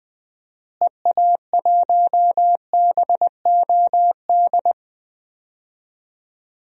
Morse code, heard as a keyed tone, about 20 wpm, 700 Hz. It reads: EA1BOD
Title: DXCC-100 Call Signs - 6 Characters - Repeated 2x - 20wpm